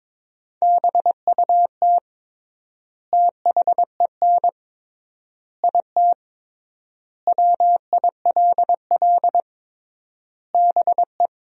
Morse code, heard as a keyed tone, about 22 wpm, 700 Hz. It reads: BUT THEN IT WILL BE
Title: 3x Word-Spacing - Sentences from Top 300 Words - 22wpm